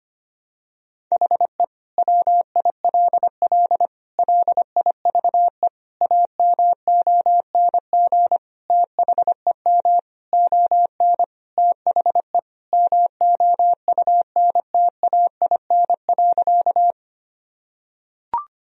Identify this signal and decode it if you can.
Morse code, heard as a keyed tone, about 25 wpm, 700 Hz. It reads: HE WILL LIVE AMONG THEM ON THE MOUNTAIN.
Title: Sentences from Top 1000 Words - Review Part 3 - 25wpm